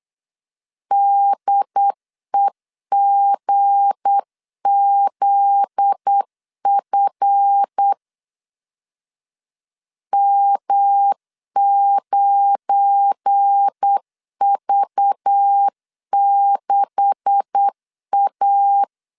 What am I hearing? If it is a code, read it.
Morse code, heard as a keyed tone, about 8 wpm, 800 Hz. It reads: DEGZF M9V6A
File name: LEZIONI / 9 286 8wpm 3425